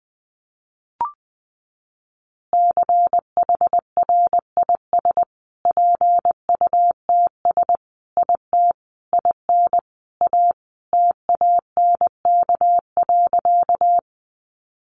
Morse code, heard as a keyed tone, about 19 wpm, 700 Hz. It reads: CHRIS PUTS IT IN A TANK.
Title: Easy Phrases 20wpm